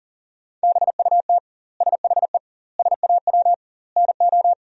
Morse code, heard as a keyed tone, about 40 wpm, 700 Hz. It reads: BUT SHE SAW NO
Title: 2x Word-Spacing - Sentences from Top 300 Words - 40wpm